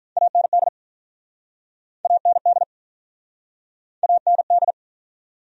Morse code, heard as a keyed tone, about 48 wpm, 700 Hz. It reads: AND AND AND